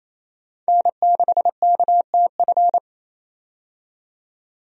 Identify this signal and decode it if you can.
Morse code, heard as a keyed tone, about 28 wpm, 700 Hz. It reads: N6KTF
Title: USA Call Signs - 5 Characters - 28wpm